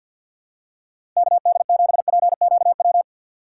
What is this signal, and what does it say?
Morse code, heard as a keyed tone, about 53 wpm, 700 Hz. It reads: KD7PQW
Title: USA Call Signs - 6x Character Spacing - 50wpm